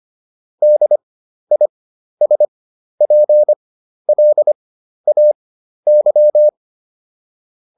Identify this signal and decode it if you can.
Morse code, heard as a keyed tone, about 25 wpm, spaced as for 12 wpm, 600 Hz. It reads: DISPLAY